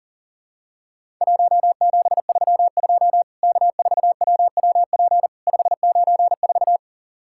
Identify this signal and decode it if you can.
Morse code, heard as a keyed tone, about 40 wpm, 700 Hz. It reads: 1732 K4WWP 594
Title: Worked All Europe DX Contest - QTC Exchange - 40wpm